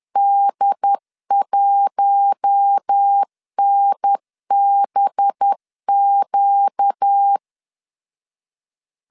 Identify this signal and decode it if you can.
Morse code, heard as a keyed tone, about 10 wpm, 800 Hz. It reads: D1NBQ
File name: CW / DET / 1 6 1500C S B 10wpm 4137